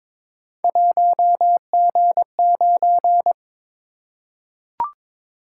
Morse code, heard as a keyed tone, about 22 wpm, 700 Hz. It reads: 1G9
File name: practice / 22wpm / Three Letter-Number Combinations - 22wpm